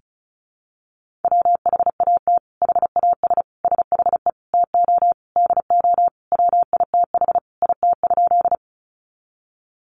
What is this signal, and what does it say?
Morse code, heard as a keyed tone, about 34 wpm, 700 Hz. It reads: WHAT HAS SHE TO DO WITH IT?
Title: Sentences from Top 100 Words 35wpm